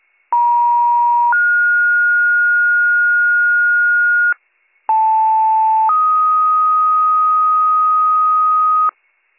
Two-tone page, heard as a one-second tone, then a three-second tone, about 950 then 1430 Hz, about 880 then 1250 Hz, with noise. It is heard from 0.3 to 8.9 s.